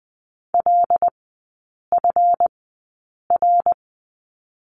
Morse code, heard as a keyed tone, about 20 wpm, spaced as for 8 wpm, 700 Hz. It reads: LFR
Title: 159 Lesson 26e - Letter g - Three Characters 20wpm